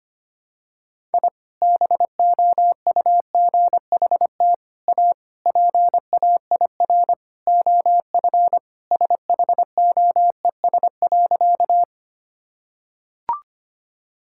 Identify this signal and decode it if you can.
Morse code, heard as a keyed tone, about 25 wpm, 700 Hz. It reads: I BOUGHT A PAIR OF SHOES.
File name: practice / 25wpm / Sentences with 3-Word Phrases - 25wpm